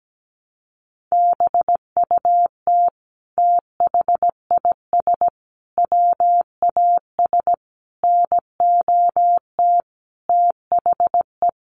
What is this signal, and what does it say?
Morse code, heard as a keyed tone, about 17 wpm, 700 Hz. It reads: BUT THIS WAS NOT THE
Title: Sentences from Top 700 Words 17wpm